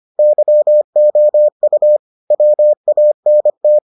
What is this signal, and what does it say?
Morse code, heard as a keyed tone, about 25 wpm, 600 Hz. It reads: YOU WANT